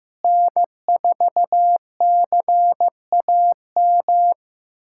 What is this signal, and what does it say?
Morse code, heard as a keyed tone, about 15 wpm, 700 Hz. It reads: N4CAM
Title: USA Callsign DE Callsign - 5 Characters - 15wpm